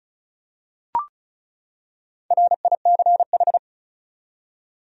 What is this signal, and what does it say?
Morse code, heard as a keyed tone, about 34 wpm, 700 Hz. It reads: RICH
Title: USA Common Names - Max 4 Letters - 35wpm